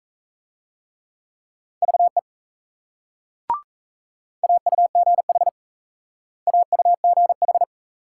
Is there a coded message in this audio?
Yes: Morse code from 1.8 to 7.6 s, heard as a keyed tone, about 36 wpm, 700 Hz. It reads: VE AUGH AUGH